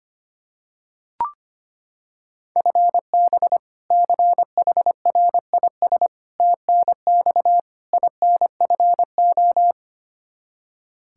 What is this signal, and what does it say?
Morse code, heard as a keyed tone, about 25 wpm, 700 Hz. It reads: FB CHRIS TNX INFO
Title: Practice QSO - Exchange 3 - More Info - 25wpm